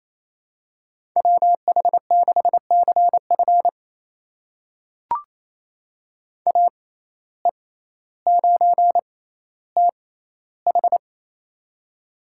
Morse code, heard as a keyed tone, about 28 wpm, spaced as for 14 wpm, 700 Hz. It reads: WH6CF AE9TH
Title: USA Call Signs - 5 Characters - 6x Character Spacing - 28wpm